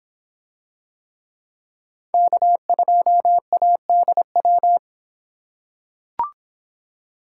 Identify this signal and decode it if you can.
Morse code, heard as a keyed tone, about 26 wpm, 700 Hz. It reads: K2ADW